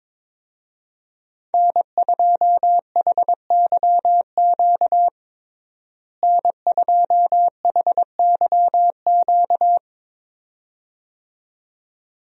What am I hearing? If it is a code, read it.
Morse code, heard as a keyed tone, about 22 wpm, 700 Hz. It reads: N2HYQ N2HYQ
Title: USA Call Signs - 5 Characters - Repeated 2x - 22wpm